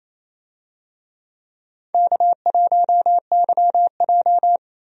Morse code, heard as a keyed tone, about 28 wpm, 700 Hz. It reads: K1YJ
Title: USA Call Signs - 4 Characters - 28wpm